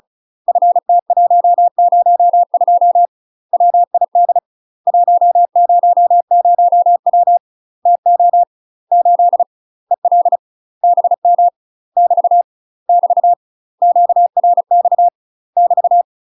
Morse code, heard as a keyed tone, about 34 wpm, 700 Hz. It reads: FT102 WID 100W TO 8 EL BM <BT> <BT> QRX <BT>
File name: UK MASTER QSOs TRAINING / 35WPM / UK MASTER QSOs 0001